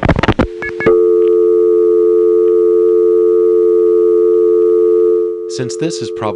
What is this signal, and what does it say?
Dial tone (North America), 350 + 440 Hz, with speech.
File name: phonetrips / SmyrnaGA5HQ